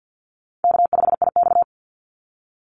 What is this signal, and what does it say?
Morse code, heard as a keyed tone, about 50 wpm, 700 Hz. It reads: K5IX
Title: Easy Call Signs - 50wpm